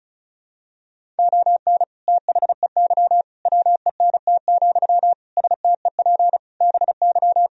Morse code, heard as a keyed tone, about 36 wpm, 700 Hz. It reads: ON THEY WENT, STEP BY
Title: Sentences from Top 1000 Words - Review Part 2 - 35wpm